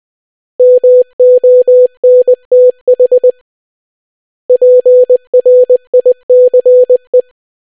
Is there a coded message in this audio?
Yes: Morse code from 0.6 to 7.2 s, heard as a keyed tone, about 20 wpm, 500 Hz. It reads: MONTH PRICE